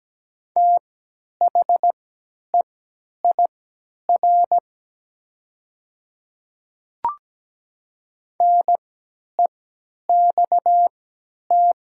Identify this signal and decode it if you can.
Morse code, heard as a keyed tone, about 17 wpm, spaced as for 10 wpm, 700 Hz. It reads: THEIR NEXT
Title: 3x Character Spacing - Top 300 Words - 17wpm